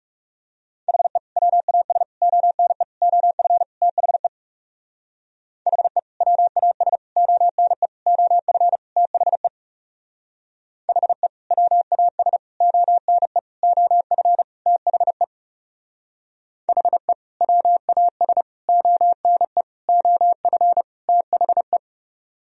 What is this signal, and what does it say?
Morse code, heard as a keyed tone, about 37 wpm, 700 Hz. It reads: HE WAS ONE OF THE HE WAS ONE OF THE HE WAS ONE OF THE HE WAS ONE OF THE
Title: Speed-Racing - Sets of 5 Words 30wpm